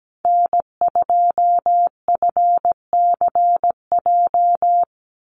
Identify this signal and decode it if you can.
Morse code, heard as a keyed tone, about 17 wpm, 700 Hz. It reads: N2FCJ
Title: USA Call Signs 17wpm